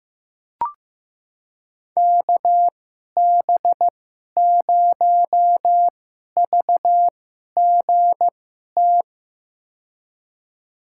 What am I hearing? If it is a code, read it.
Morse code, heard as a keyed tone, about 15 wpm, spaced as for 11 wpm, 700 Hz. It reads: KB0VGT
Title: USA Call Signs - 6 Characters - 2x Character Spacing - 15wpm